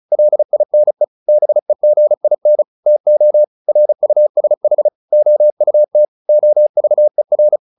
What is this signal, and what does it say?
Morse code, heard as a keyed tone, about 34 wpm, 600 Hz. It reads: LINE BEGIN TO RUSH OUT OVER
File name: the old man and the sea / the old man and the sea 1 35wpm